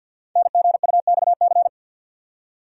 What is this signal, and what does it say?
Morse code, heard as a keyed tone, about 45 wpm, 700 Hz. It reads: NGUXC
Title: Code Groups - 5 Letter:Numbers - 50wpm